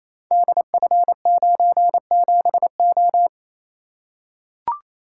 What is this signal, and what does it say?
Morse code, heard as a keyed tone, about 29 wpm, 700 Hz. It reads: DF97O
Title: Code Groups - 5 Letter:Numbers - 28wpm